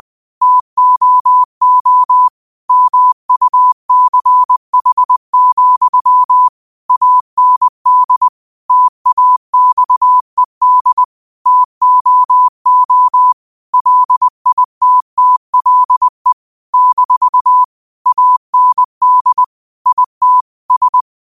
Morse code, heard as a keyed tone, about 20 wpm, 1000 Hz. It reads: TOO MUCH, AND TAXED TOO LITTLE - AND IT S
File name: qotdfiles / 20wpm